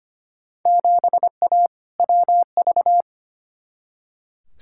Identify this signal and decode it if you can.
Morse code, heard as a keyed tone, about 25 wpm, 700 Hz. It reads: 7A WV
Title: Field Day Exchanges Encore - 25wpm